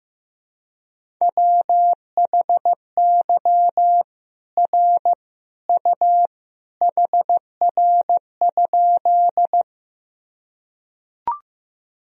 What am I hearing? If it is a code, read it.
Morse code, heard as a keyed tone, about 15 wpm, 700 Hz. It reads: WHY R U HR?
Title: Sentences from Top 200 Words and Abbreviations 15wpm